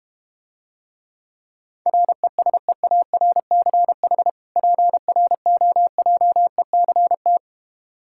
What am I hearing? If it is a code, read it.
Morse code, heard as a keyed tone, about 32 wpm, 700 Hz. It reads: RESEARCH PROJECT